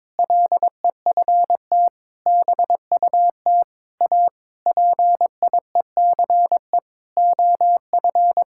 Morse code, heard as a keyed tone, about 22 wpm, 700 Hz. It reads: LEFT BUT A PIECE OF
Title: Sentences from Top 1600 Words 22wpm